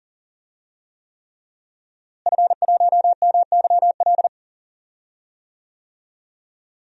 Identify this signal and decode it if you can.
Morse code, heard as a keyed tone, about 40 wpm, 700 Hz. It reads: F1MYL
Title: DXCC-100 Call Signs - 5 Characters - Repeated 3x - 40wpm